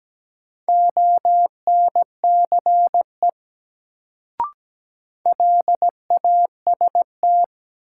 Morse code, heard as a keyed tone, about 17 wpm, 700 Hz. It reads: ONCE LAST
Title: Sentences from Top 300 Words - New Word Drill - 17wpm